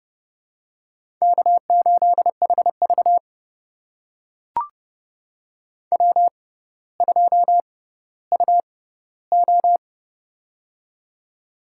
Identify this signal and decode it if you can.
Morse code, heard as a keyed tone, about 31 wpm, spaced as for 15 wpm, 700 Hz. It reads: K8HV W2UO
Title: USA Call Signs - 4 Characters - 6x Character Spacing - 30wpm